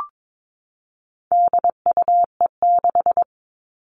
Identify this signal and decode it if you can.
Morse code, heard as a keyed tone, about 22 wpm, 700 Hz. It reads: DUE6
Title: All Cut Numbers - 4-Digit 22wpm